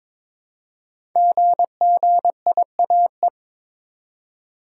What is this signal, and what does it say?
Morse code, heard as a keyed tone, about 22 wpm, 690 Hz. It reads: GGIAE
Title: Code Groups - 5 Letters - 22wpm